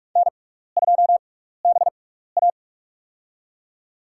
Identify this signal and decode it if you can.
Morse code, heard as a keyed tone, about 46 wpm, spaced as for 15 wpm, 700 Hz. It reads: N2BA
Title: USA Call Signs - 4 Characters - 6x Character Spacing - 45wpm